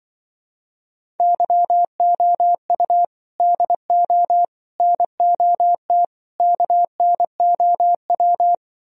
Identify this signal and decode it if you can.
Morse code, heard as a keyed tone, about 25 wpm, 700 Hz. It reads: YOU DO NOT KNOW
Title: Speed-Racing - Sets of 4 Words 20wpm